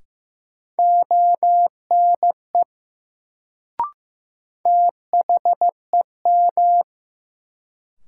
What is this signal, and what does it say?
Morse code, heard as a keyed tone, about 15 wpm, 700 Hz. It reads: ONE THEM
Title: Sets of 2 Words - New Word Drill - 15wpm